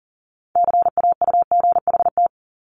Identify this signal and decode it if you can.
Morse code, heard as a keyed tone, about 38 wpm, 700 Hz. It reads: CAUGHT
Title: Multi-letter Phonemes 40wpm